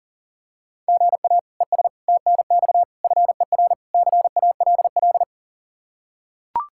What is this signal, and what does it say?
Morse code, heard as a keyed tone, about 40 wpm, 700 Hz. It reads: GA ES TNX FER CALL
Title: Practice QSO - Exchange 1 - Beginning - 40wpm